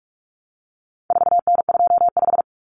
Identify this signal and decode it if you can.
Morse code, heard as a keyed tone, about 44 wpm, 700 Hz. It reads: 4N25
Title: Common Cut Numbers - 4-Digit 45wpm